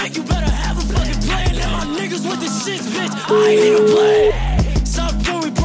Ringback tone (Europe, 425 Hz), with noise.